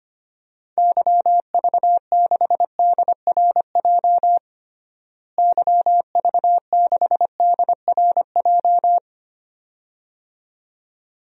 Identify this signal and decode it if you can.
Morse code, heard as a keyed tone, about 25 wpm, 700 Hz. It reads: YV6DRJ YV6DRJ
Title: DXCC-100 Call Signs - 6 Characters - Repeated 2x - 25wpm